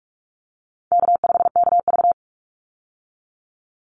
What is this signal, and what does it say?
Morse code, heard as a keyed tone, about 44 wpm, 700 Hz. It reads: K5KV
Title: Easy Call Signs Repeated 3x - 45wpm